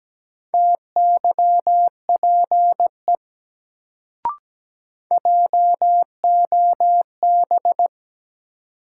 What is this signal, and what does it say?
Morse code, heard as a keyed tone, about 17 wpm, 700 Hz. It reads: TYPE JOB
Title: Sentences from Top 600 Words - New Word Drill - 17wpm